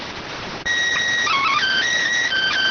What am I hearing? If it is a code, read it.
SSTV (slow-scan television), PD 290.